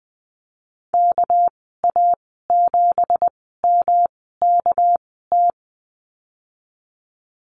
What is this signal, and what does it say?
Morse code, heard as a keyed tone, about 20 wpm, spaced as for 15 wpm, 700 Hz. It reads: KA7MKT
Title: 231 Lesson 37g - Letter x - Call Signs 20wpm